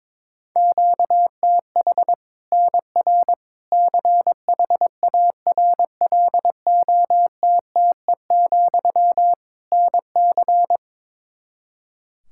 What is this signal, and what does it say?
Morse code, heard as a keyed tone, about 21 wpm, 700 Hz. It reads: QTH NR CHARLOTTE, NC
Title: Practice QSO - Exchange 1 - Beginning - 22wpm